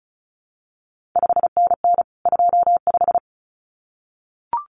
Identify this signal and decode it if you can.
Morse code, heard as a keyed tone, about 36 wpm, 700 Hz. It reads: 5NN 25
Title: CQ WW DX Contest Exchange - 35wpm